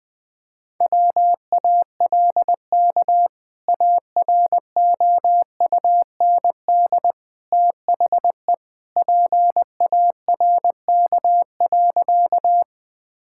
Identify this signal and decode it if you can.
Morse code, heard as a keyed tone, about 20 wpm, 700 Hz. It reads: WALK AROUND THE PARK.
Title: Sentences with 2-Word Phrases - Spoken Beginning - 20wpm